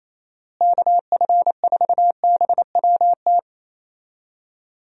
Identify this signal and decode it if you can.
Morse code, heard as a keyed tone, about 27 wpm, 700 Hz. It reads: KF4BWT
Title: USA Call Signs - 6 Characters - Repeated 3x - 28wpm